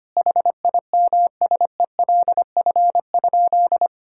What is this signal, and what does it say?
Morse code, heard as a keyed tone, about 25 wpm, 700 Hz. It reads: HIMSELF?
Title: Sentences from Top 1600 Words 25wpm